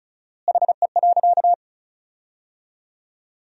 Morse code, heard as a keyed tone, about 36 wpm, 700 Hz. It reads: HE.
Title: 4x Word-Spacing - Sentences from Top 300 Words - 35wpm